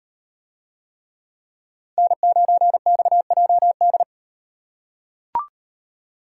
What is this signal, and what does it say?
Morse code, heard as a keyed tone, about 38 wpm, 700 Hz. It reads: N9XJD